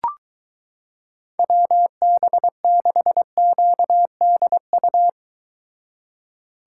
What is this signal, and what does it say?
Morse code, heard as a keyed tone, about 23 wpm, 700 Hz. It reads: WB6QDU